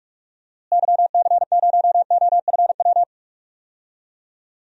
Morse code, heard as a keyed tone, about 42 wpm, 700 Hz. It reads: YC0OFW
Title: DXCC-100 Call Signs - 6 Characters - Repeated 3x - 45wpm